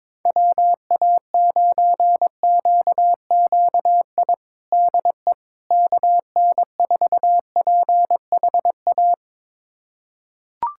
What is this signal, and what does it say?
Morse code, heard as a keyed tone, about 22 wpm, 700 Hz. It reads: WA9QQI DE KN4PHA
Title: USA Callsign DE Callsign 22wpm